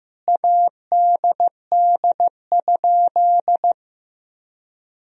Morse code, heard as a keyed tone, about 15 wpm, 700 Hz. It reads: ADD?
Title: Sentences from Top 1000 Words - Review Part 1 - 15wpm